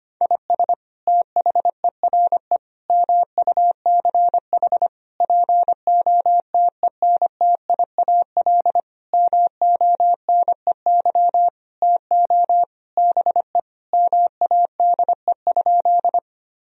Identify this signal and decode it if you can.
Morse code, heard as a keyed tone, about 25 wpm, 700 Hz. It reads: IS THERE MUCH POTENTIAL MONEY TO BE MADE?
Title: Sentences from Top 1000 Words - Review Part 4 - 25wpm